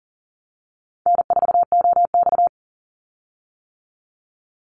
Morse code, heard as a keyed tone, about 42 wpm, 700 Hz. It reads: N4OX
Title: Easy Call Signs - 40wpm